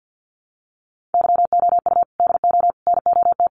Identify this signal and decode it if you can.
Morse code, heard as a keyed tone, about 50 wpm, 700 Hz. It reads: YOU DO NOT
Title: Sets of 5 Words 50wpm